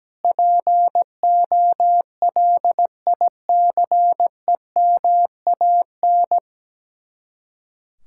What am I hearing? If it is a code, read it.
Morse code, heard as a keyed tone, about 17 wpm, 700 Hz. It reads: POLICEMAN
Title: Speed-Racing Common QSO Phrases - 17wpm